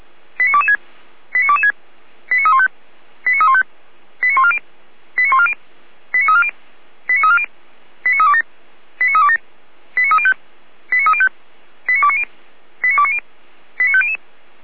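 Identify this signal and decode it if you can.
Five-tone selective call (ZVEI), heard as quick runs of five tones, tones of about 70 ms. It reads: 89226 89226 89315 89315 89140 89140 89340 89340 89327 89327 89305 89305 89290 89290 89500
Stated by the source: ZVEI five-tone selective call received with an SDR and recorded by Feuerwehr Orlamünde.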